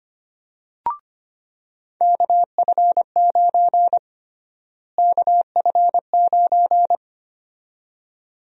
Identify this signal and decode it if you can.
Morse code, heard as a keyed tone, about 25 wpm, 700 Hz. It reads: KF9 KF9
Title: USA Call Sign Prefixes - Repeated 2x - 25wpm